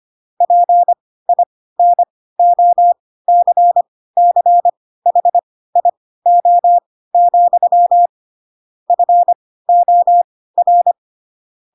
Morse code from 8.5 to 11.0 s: FOR